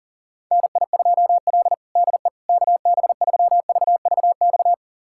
Morse code, heard as a keyed tone, about 40 wpm, 700 Hz. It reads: NI2L DE KB3VVX
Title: USA Callsign DE Callsign 40wpm